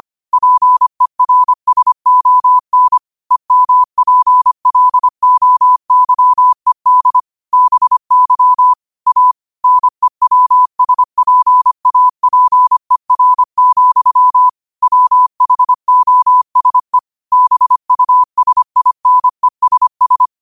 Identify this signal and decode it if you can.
Morse code, heard as a keyed tone, about 25 wpm, 1000 Hz. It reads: PERSON EMPLOYED BY A NEWSPAPER, WHOSE BUSINESS
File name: qotdfiles / 25wpm